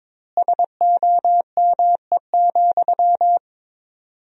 Morse code, heard as a keyed tone, about 21 wpm, 700 Hz. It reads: SOME,
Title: 4x Word-Spacing - Sentences from Top 200 Words - 22wpm